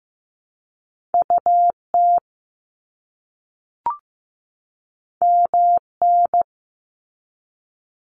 Morse code, heard as a keyed tone, about 15 wpm, 700 Hz. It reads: UT MN
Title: US States 15wpm